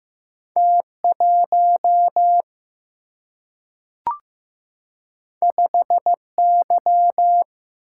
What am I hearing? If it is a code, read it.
Morse code, heard as a keyed tone, about 15 wpm, 700 Hz. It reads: T1 5Y
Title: Single-Letter Single-Number 15wpm